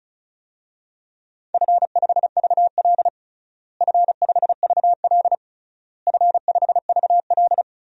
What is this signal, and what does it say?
Morse code, heard as a keyed tone, about 34 wpm, 700 Hz. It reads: F5VL F5VL F5VL
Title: DXCC-100 Call Signs - 4 Characters - Repeated 3x - 35wpm